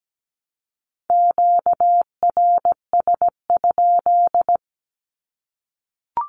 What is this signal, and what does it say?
Morse code, heard as a keyed tone, about 17 wpm, 700 Hz. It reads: QRS?
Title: Q-Codes 17wpm